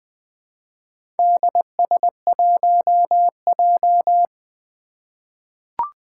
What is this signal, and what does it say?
Morse code, heard as a keyed tone, about 20 wpm, 700 Hz. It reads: DS1J
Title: Four Challenging Letter-Number Combinations 20wpm